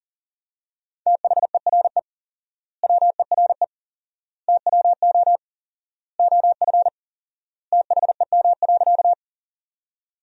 Morse code, heard as a keyed tone, about 40 wpm, 700 Hz. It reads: THERE WERE TWO OF THEM.
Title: 4x Word-Spacing - Sentences from Top 100 Words - 40wpm